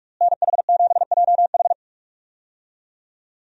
Morse code, heard as a keyed tone, about 46 wpm, 700 Hz. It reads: NH7JH
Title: USA Call Signs - 5 Characters - Repeated 2x - 45wpm